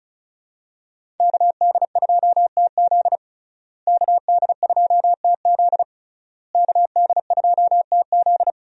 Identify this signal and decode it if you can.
Morse code, heard as a keyed tone, about 34 wpm, 700 Hz. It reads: KD2TZ KD2TZ KD2TZ
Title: USA Call Signs - 5 Characters - Repeated 3x - 35wpm